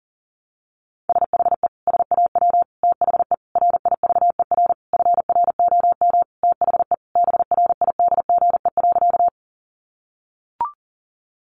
Morse code, heard as a keyed tone, about 40 wpm, 700 Hz. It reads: SHE SAW THE RIVER FROM THE BRIDGE.
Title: Sentences from Top 1200 Words 40wpm